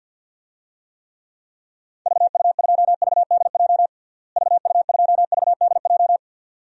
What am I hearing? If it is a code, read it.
Morse code, heard as a keyed tone, about 52 wpm, 700 Hz. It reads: VU2VDJ VU2VDJ
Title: DXCC-100 Call Signs - 6 Characters - Repeated 2x - 50wpm